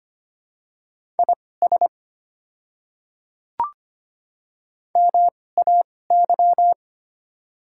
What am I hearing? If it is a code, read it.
Morse code, heard as a keyed tone, about 25 wpm, spaced as for 18 wpm, 700 Hz. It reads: IS MAY
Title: 2x Character Spacing - Top 100 Words - 25wpm